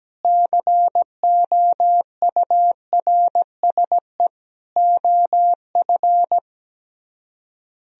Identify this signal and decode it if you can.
Morse code, heard as a keyed tone, about 17 wpm, 700 Hz. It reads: COURSE OF
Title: Speed-Racing - Sets of 4 Words 17wpm